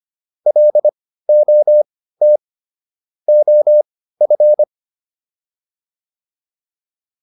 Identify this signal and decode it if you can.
Morse code, heard as a keyed tone, about 25 wpm, spaced as for 15 wpm, 600 Hz. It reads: LOT OF
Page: intermediate-practice-files